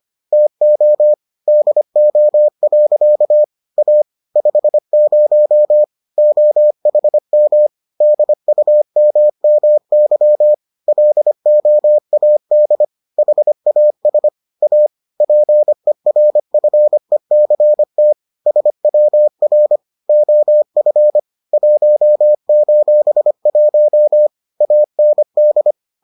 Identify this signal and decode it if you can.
Morse code, heard as a keyed tone, about 25 wpm, 600 Hz. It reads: TO DO. A 50 OHM DUMMY LOAD HAS A PERFECT SWR OF 1:1 AND